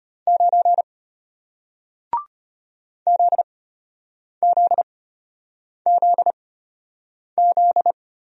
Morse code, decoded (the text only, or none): 9 ZZZZ